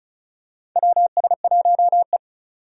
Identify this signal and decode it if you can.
Morse code, heard as a keyed tone, about 36 wpm, 700 Hz. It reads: WS1E
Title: USA Call Signs - 4 Characters - 35wpm